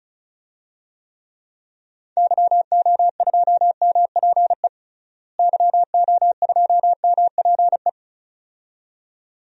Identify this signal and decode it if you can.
Morse code, heard as a keyed tone, about 34 wpm, 700 Hz. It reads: YO2MPE YO2MPE
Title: DXCC-100 Call Signs - 6 Characters - Repeated 2x - 35wpm